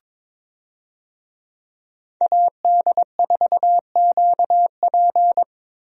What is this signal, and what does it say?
Morse code, heard as a keyed tone, about 22 wpm, 700 Hz. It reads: AD4QP
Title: USA Call Signs - 5 Characters - Repeated 3x - 22wpm